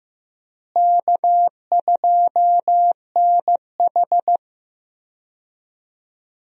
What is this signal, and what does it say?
Morse code, heard as a keyed tone, about 15 wpm, 700 Hz. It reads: K2NH
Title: USA Call Signs - 4 Characters - Repeated 3x - 15wpm